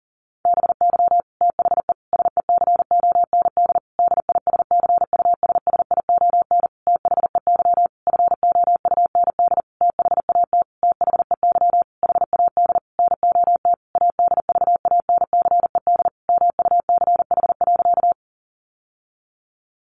Morse code, heard as a keyed tone, about 40 wpm, 700 Hz. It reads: BY THE SECOND DISCUSSION THEY FOUND THAT THEY HAD NOT ADVANCED MUCH.
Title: Sentences from Top 1200 Words 40wpm